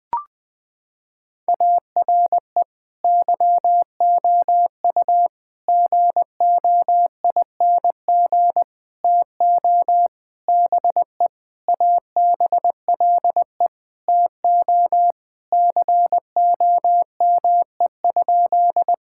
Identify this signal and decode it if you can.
Morse code, heard as a keyed tone, about 20 wpm, 700 Hz. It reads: ARE YOU GOING TO BE ABLE TO COME?
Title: Sentences with 5-Word Phrases - 20wpm